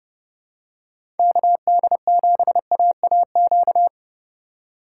Morse code, heard as a keyed tone, about 30 wpm, 700 Hz. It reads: KD7AAQ
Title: USA Call Signs - 6 Characters - 4x Character Spacing - 30wpm